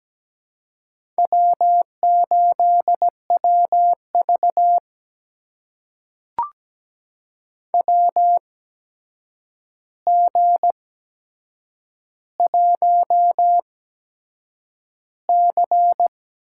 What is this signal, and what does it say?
Morse code, heard as a keyed tone, about 17 wpm, spaced as for 7 wpm, 700 Hz. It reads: W8WV WG1C
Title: USA Call Signs - 4 Characters - 8x Character Spacing - 17wpm